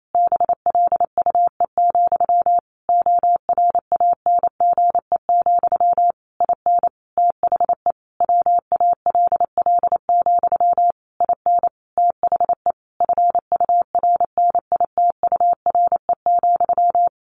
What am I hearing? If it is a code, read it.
Morse code, heard as a keyed tone, about 28 wpm, 700 Hz. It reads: BLUE, ORANGE, IN THE WALL, IN THE FURNITURE,
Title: A Sound Of Thunder 28wpm